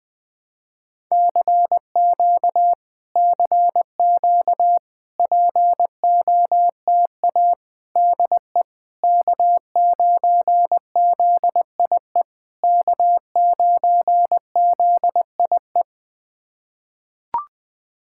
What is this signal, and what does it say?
Morse code, heard as a keyed tone, about 20 wpm, 690 Hz. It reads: CQ CQ POTA DE K9ZIE K9ZIE
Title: POTA QSO Practice - 20wpm